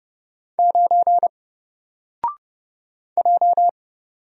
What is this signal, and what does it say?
Morse code, heard as a keyed tone, about 30 wpm, 700 Hz. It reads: 9 J
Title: Single Letter-Number 30wpm